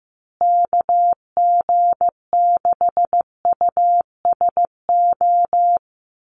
Morse code, heard as a keyed tone, about 15 wpm, 700 Hz. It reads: KG6USO
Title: USA Call Signs Repeated 3x 15wpm